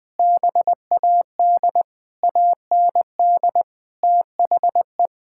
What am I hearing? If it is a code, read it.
Morse code, heard as a keyed tone, about 20 wpm, 700 Hz. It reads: BAD AND THE
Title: Sentences with 7-Word Phrases - 20wpm